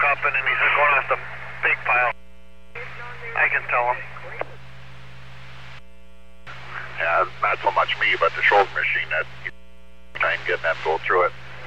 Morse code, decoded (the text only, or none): none